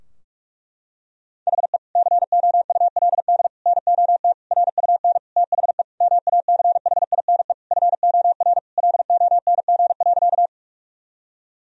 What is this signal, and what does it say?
Morse code, heard as a keyed tone, about 46 wpm, 700 Hz. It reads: HE COULD NOT RUN THE MACHINE FOR LONG.
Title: Sentences from Top 1000 Words - Review Part 2 - 45wpm